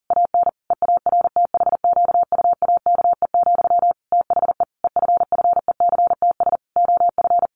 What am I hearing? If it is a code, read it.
Morse code, heard as a keyed tone, about 40 wpm, 700 Hz. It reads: AN EARTHQUAKE, THE EFFECTS OF
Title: A Sound Of Thunder 40wpm